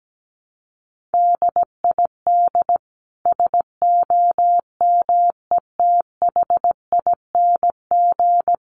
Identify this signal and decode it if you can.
Morse code, heard as a keyed tone, about 17 wpm, 700 Hz. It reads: DID SOMETHING
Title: Sentences from Top 600 Words 17wpm